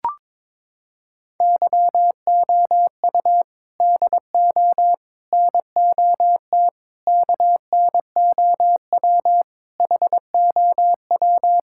Morse code, heard as a keyed tone, about 22 wpm, 700 Hz. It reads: YOU DO NOT KNOW HOW
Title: Sentences from Top 1000 Words - Review Part 6 - 22wpm